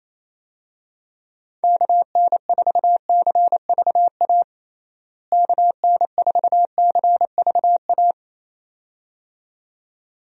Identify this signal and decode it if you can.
Morse code, heard as a keyed tone, about 28 wpm, 700 Hz. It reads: KN4CVA KN4CVA
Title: USA Call Signs - 6 Characters - Repeated 2x - 28wpm